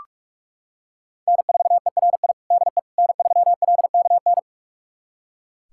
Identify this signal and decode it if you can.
Morse code, heard as a keyed tone, about 41 wpm, 700 Hz. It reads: N4ERI DE N3LKN
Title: USA Callsign DE Callsign - 5 Characters - 45wpm